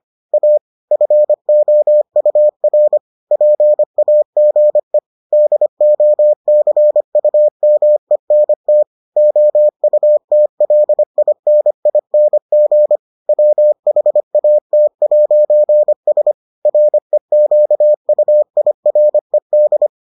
Morse code, heard as a keyed tone, about 25 wpm, 600 Hz. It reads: A FOUR PAGE DOCUMENT OUTLINING WHAT'S REQUIRED